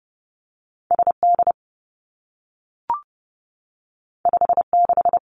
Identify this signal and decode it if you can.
Morse code, heard as a keyed tone, about 30 wpm, 700 Hz. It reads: S D 5 6
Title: Two Challenging Letter-Number Combinations 30wpm